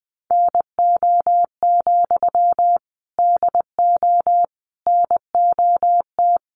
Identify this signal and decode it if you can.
Morse code, heard as a keyed tone, about 20 wpm, 700 Hz. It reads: NO, DO NOT
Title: Sentences from Top 1200 Words 20wpm